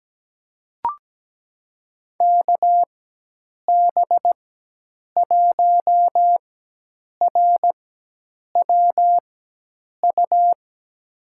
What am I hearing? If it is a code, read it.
Morse code, heard as a keyed tone, about 17 wpm, spaced as for 8 wpm, 700 Hz. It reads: KB1RWU
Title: USA Call Signs - 4x Character Spacing - 17wpm